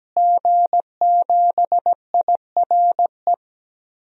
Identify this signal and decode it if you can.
Morse code, heard as a keyed tone, about 17 wpm, 700 Hz. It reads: G7IRE
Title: DXCC-100 Call Signs - 5 Characters - Repeated 2x - 17wpm